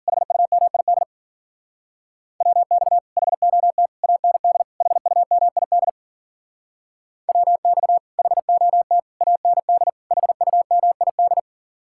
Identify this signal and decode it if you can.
Morse code, heard as a keyed tone, about 52 wpm, 700 Hz. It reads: HUMID WX HOT AND HUMID WX HOT AND HUMID